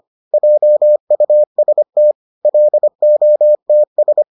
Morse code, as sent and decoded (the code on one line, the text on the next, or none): .--- ..- ... - / .-.. --- - ...
JUST LOTS